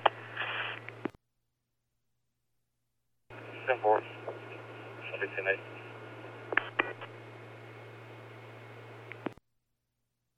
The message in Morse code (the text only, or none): none